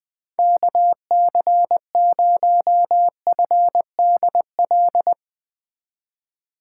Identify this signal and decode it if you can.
Morse code, heard as a keyed tone, about 21 wpm, 700 Hz. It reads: KC0FDL
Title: USA Call Signs - 6 Characters - 2x Character Spacing - 20wpm